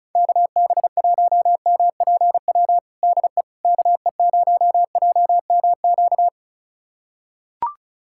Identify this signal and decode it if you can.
Morse code, heard as a keyed tone, about 34 wpm, 700 Hz. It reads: KB1MPW DE KE0JMQ
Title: USA Callsign DE Callsign - 6 Characters - 35wpm